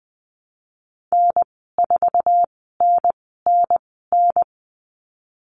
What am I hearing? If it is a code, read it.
Morse code, heard as a keyed tone, about 20 wpm, spaced as for 14 wpm, 700 Hz. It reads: N4NNN
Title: 069 Lesson 12g - Letter d - Call Signs 20wpm